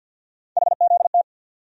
Morse code, heard as a keyed tone, about 48 wpm, 700 Hz. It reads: HZT
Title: USA Call Sign Postfixes - 3 Letters 50wpm